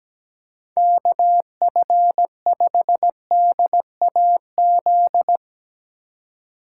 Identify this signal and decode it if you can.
Morse code, heard as a keyed tone, about 17 wpm, 700 Hz. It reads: KF5DAZ
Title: Speed Racing - USA 2x3 Call Signs 17wpm